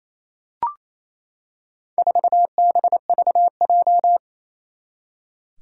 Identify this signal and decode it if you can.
Morse code, heard as a keyed tone, about 28 wpm, 700 Hz. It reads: 4BVJ
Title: Four Challenging Letter-Number Combinations 28wpm